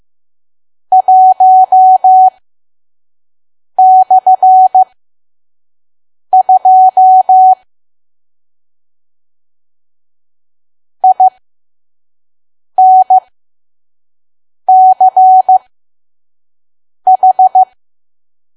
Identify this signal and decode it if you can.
Morse code, heard as a keyed tone, about 15 wpm, spaced as for 5 wpm, 750 Hz. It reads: 1/2 INCH
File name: Archive / 5 WPM files / PM